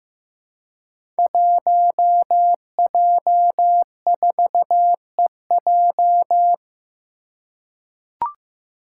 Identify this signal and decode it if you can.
Morse code, heard as a keyed tone, about 15 wpm, 700 Hz. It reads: 1J4EJ
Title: Five Challenging Letter-Number Combinations 15wpm